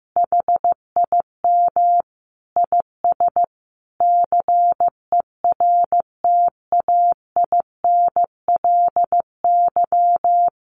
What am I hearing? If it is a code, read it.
Morse code, heard as a keyed tone, about 14 wpm, 700 Hz. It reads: HIM IS CERTAINLY
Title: Sentences from Top 1300 Words 15wpm